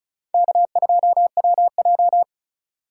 Morse code, heard as a keyed tone, about 34 wpm, 700 Hz. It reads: K2WJ
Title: USA Call Signs - 4 Characters - 4x Character Spacing - 35wpm